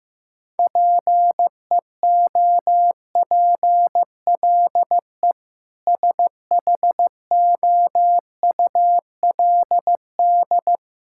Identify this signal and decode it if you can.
Morse code, heard as a keyed tone, about 15 wpm, 700 Hz. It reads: PEOPLE SHOULD
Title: Sentences from Top 1000 Words - Review Part 6 - 15wpm